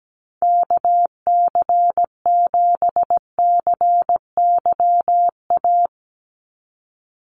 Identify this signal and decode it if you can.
Morse code, heard as a keyed tone, about 17 wpm, 700 Hz. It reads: KC7CYA
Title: USA Call Signs 17wpm